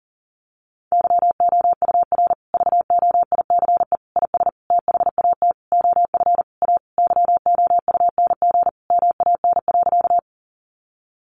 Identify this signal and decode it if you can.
Morse code, heard as a keyed tone, about 40 wpm, 700 Hz. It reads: YOUR VOICE IS THAT OF A YOUNG MAN.
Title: Sentences from Top 500 Words 40wpm